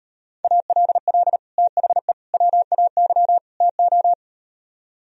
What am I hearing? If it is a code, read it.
Morse code, heard as a keyed tone, about 38 wpm, 700 Hz. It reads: ALL THE WAY TO